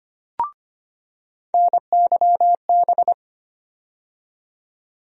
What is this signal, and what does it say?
Morse code, heard as a keyed tone, about 25 wpm, 700 Hz. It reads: NYB
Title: USA Call Sign Postfixes - 3 Letters 25wpm